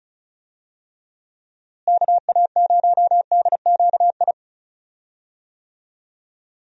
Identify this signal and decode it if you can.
Morse code, heard as a keyed tone, about 36 wpm, 700 Hz. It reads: KA0DQI
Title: USA Call Signs - 6 Characters - Repeated 2x - 35wpm